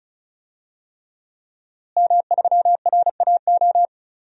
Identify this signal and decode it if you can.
Morse code, heard as a keyed tone, about 36 wpm, 700 Hz. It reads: M3RAO
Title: DXCC-100 Call Signs - 5 Characters - Repeated 3x - 35wpm